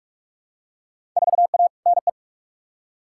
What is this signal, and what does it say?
Morse code, heard as a keyed tone, about 44 wpm, 700 Hz. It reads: 4A NE
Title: Field Day Exchanges - 45wpm